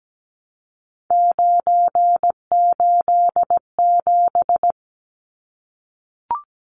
Morse code, heard as a keyed tone, about 17 wpm, 700 Hz. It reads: 987
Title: Numbers - 3-Digit 17wpm